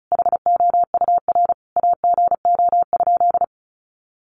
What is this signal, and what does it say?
Morse code, heard as a keyed tone, about 34 wpm, 700 Hz. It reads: HOUR AGO?
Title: Sentences from Top 1300 Words 35wpm